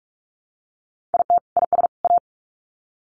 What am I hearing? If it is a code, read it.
Morse code, heard as a keyed tone, about 44 wpm, 700 Hz. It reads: IT IS A
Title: Sets of 3 Words 45wpm